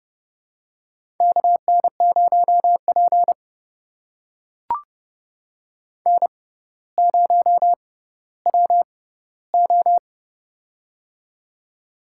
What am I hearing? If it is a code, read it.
Morse code, heard as a keyed tone, about 30 wpm, spaced as for 15 wpm, 700 Hz. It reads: KN0P N0WO
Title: USA Call Signs - 4 Characters - 6x Character Spacing - 30wpm